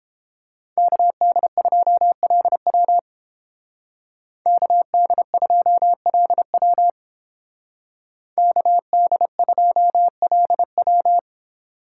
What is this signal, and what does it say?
Morse code, heard as a keyed tone, about 28 wpm, 700 Hz. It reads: KD2LW KD2LW KD2LW